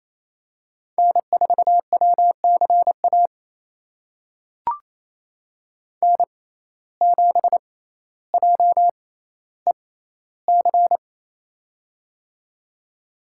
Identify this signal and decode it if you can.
Morse code, heard as a keyed tone, about 28 wpm, spaced as for 14 wpm, 700 Hz. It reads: N4WCA N7JEC
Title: USA Call Signs - 5 Characters - 6x Character Spacing - 28wpm